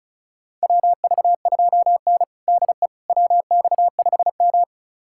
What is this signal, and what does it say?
Morse code, heard as a keyed tone, about 34 wpm, 700 Hz. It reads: WV2N DE WX5M
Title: USA Callsign DE Callsign - 4 Characters - 35wpm